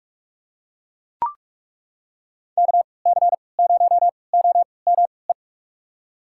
Morse code, heard as a keyed tone, about 42 wpm, spaced as for 25 wpm, 700 Hz. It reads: KC0OME
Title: USA Call Signs - 6 Characters - 3x Character Spacing - 45wpm